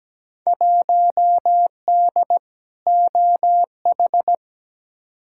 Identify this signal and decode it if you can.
Morse code, heard as a keyed tone, about 17 wpm, 700 Hz. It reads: 1D OH
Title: Field Day Exchanges - 17wpm